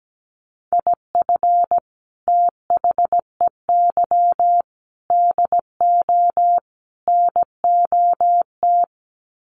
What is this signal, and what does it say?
Morse code, heard as a keyed tone, about 17 wpm, 700 Hz. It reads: IF THEY DO NOT
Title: Sets of 4 Words 17wpm